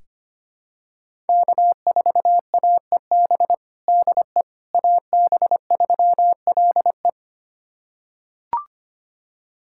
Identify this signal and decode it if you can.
Morse code, heard as a keyed tone, about 25 wpm, 700 Hz. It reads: K4AEB DE AB3LE
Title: USA Callsign DE Callsign - 5 Characters - 25wpm